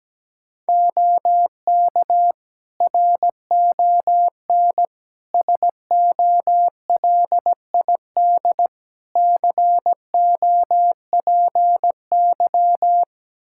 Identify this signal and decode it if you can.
Morse code, heard as a keyed tone, about 17 wpm, 700 Hz. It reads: OK RON SOLID COPY
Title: Practice QSO - Exchange 3 - More Info - 17wpm